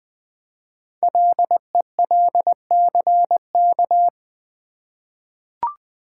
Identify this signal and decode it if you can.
Morse code, heard as a keyed tone, about 20 wpm, 700 Hz. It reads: LELCK
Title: Code Groups - 5 Letters - 20wpm